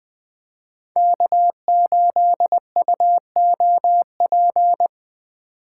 Morse code, heard as a keyed tone, about 20 wpm, 700 Hz. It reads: K8UOP